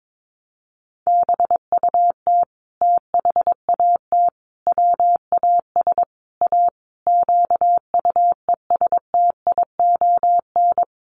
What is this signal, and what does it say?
Morse code, heard as a keyed tone, about 22 wpm, 700 Hz. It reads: BUT THAT WAS A QUESTION
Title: Sentences from Top 1100 Words 22wpm